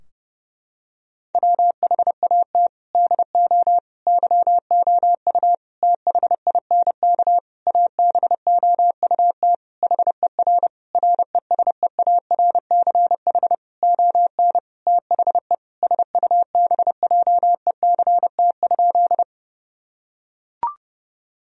Morse code, heard as a keyed tone, about 32 wpm, 700 Hz. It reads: WHAT DO YOU THINK ABOUT HER RESEARCH ON THE SUBJECT?
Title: Sentences from Top 1000 Words - Review Part 4 - 30wpm